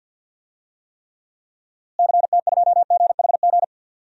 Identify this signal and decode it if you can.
Morse code, heard as a keyed tone, about 48 wpm, 700 Hz. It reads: CT2GHG
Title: DXCC-100 Call Signs - 6 Characters - Repeated 3x - 50wpm